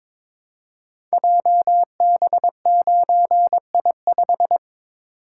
Morse code, heard as a keyed tone, about 22 wpm, 700 Hz. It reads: JB9I5
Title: Code Groups - 5 Letter:Numbers - 22wpm